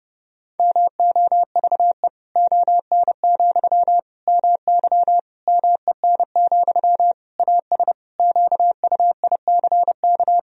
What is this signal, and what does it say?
Morse code, heard as a keyed tone, about 30 wpm, 700 Hz. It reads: MOVE ON, MY MEN, AS QUICK
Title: Sentences from Top 500 Words and Abbreviations 30wpm